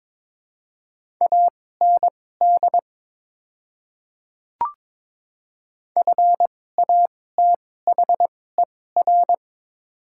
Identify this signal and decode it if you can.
Morse code, heard as a keyed tone, about 22 wpm, spaced as for 16 wpm, 690 Hz. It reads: AND FATHER
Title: 2x Character Spacing - Top 500 Words - 22wpm